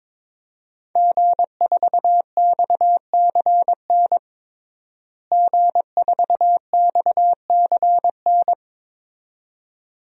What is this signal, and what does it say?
Morse code, heard as a keyed tone, about 22 wpm, 700 Hz. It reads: G4XCN G4XCN
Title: DXCC-100 Call Signs - 5 Characters - Repeated 2x - 22wpm